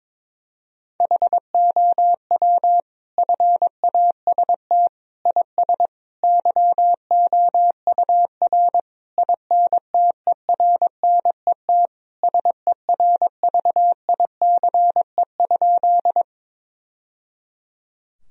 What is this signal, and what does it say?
Morse code, heard as a keyed tone, about 21 wpm, 700 Hz. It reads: HOW FAST IS YOUR INTERNET SERVICE?
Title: Sentences from Top 1000 Words - Review Part 4 - 22wpm